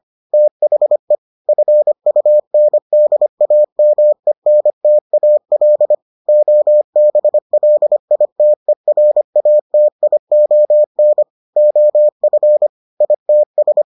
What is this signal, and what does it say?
Morse code, heard as a keyed tone, about 25 wpm, 600 Hz. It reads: THE FUNDAMENTAL OBLITERATION OF ITS